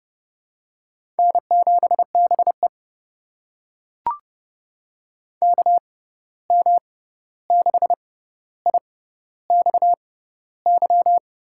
Morse code, heard as a keyed tone, about 30 wpm, spaced as for 14 wpm, 700 Hz. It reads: N7BE KM6IXY
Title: USA Call Signs - 6x Character Spacing - 30wpm